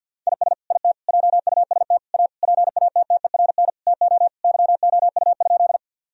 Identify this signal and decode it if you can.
Morse code, heard as a keyed tone, about 52 wpm, 700 Hz. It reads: IS IT JUST A PATTERN TO YOU?
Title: Sentences from Top 1000 Words - Review Part 2 - 50wpm